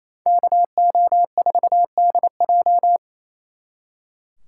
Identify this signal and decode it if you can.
Morse code, heard as a keyed tone, about 34 wpm, 700 Hz. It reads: KO4DJ
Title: Speed Racing - USA 2x2 Call Signs 28wpm